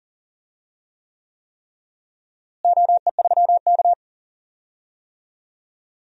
Morse code, heard as a keyed tone, about 40 wpm, 700 Hz. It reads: OE3K
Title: DXCC-100 Call Signs - 4 Characters - Repeated 3x - 40wpm